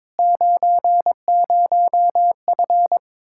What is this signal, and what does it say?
Morse code, heard as a keyed tone, about 22 wpm, 700 Hz. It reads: 90F
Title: Practice QSO - Exchange 2 - Info - 22wpm